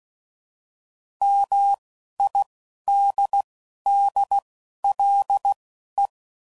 Morse code, heard as a keyed tone, about 16 wpm, spaced as for 11 wpm, 800 Hz. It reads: MIDDLE